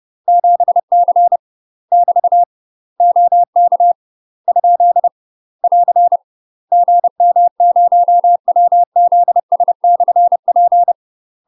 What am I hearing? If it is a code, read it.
Morse code, heard as a keyed tone, about 30 wpm, 700 Hz. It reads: 7C <BT> OK ? <AR> GM0WZS/P
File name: UK MASTER QSOs TRAINING / 30WPM / UK MASTER QSOs 0014